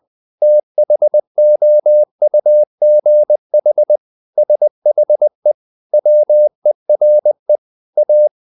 Morse code, heard as a keyed tone, about 20 wpm, 600 Hz. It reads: THOUGH SHE WERE A
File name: the old man and the sea / the old man and the sea part 1 20wpm